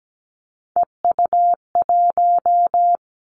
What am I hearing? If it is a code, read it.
Morse code, heard as a keyed tone, about 17 wpm, 700 Hz. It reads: EU1
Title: Callsign Prefixes for Top 50 DXCC - 17wpm